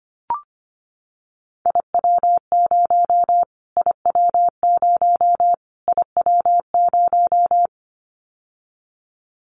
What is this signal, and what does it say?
Morse code, heard as a keyed tone, about 25 wpm, 700 Hz. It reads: IW0 IW0 IW0
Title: Callsign Prefixes for Top 25 DXCC - Repeated 3x - 25wpm